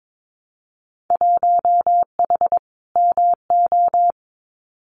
Morse code, heard as a keyed tone, about 22 wpm, 700 Hz. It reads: 1H MO
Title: Winter Field Day Exchanges 22wpm